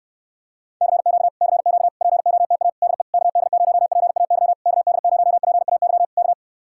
Morse code, heard as a keyed tone, about 69 wpm, 700 Hz. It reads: CQ CQ POTA DE KD0PAY KD0PAY K